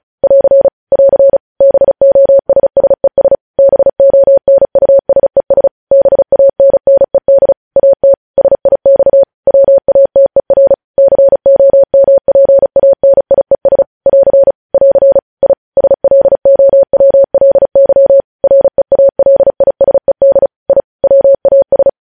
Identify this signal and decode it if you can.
Morse code, heard as a keyed tone, about 36 wpm, 550 Hz. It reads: <AR> <AR> BOSSES BONUSES BANNED AT SIX WATER COMPANIES <AR> <AR> I SLOWLY REALISED I WAS
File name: podcast / News headlines-1x-35x35